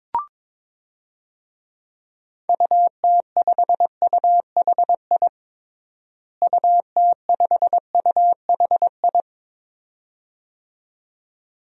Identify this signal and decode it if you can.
Morse code, heard as a keyed tone, about 22 wpm, 700 Hz. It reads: UT5UHI UT5UHI
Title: DXCC-100 Call Signs - 6 Characters - Repeated 2x - 22wpm